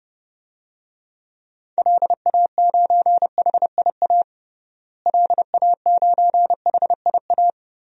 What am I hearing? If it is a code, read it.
Morse code, heard as a keyed tone, about 32 wpm, 700 Hz. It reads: LA9HIA LA9HIA
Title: DXCC-100 Call Signs - 6 Characters - Repeated 3x - 30wpm